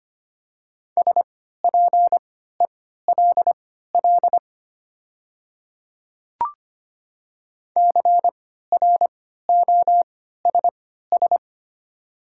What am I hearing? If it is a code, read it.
Morse code, heard as a keyed tone, about 25 wpm, spaced as for 14 wpm, 700 Hz. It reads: SPELL CROSS
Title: 3x Character Spacing - Top 300 Words - 25wpm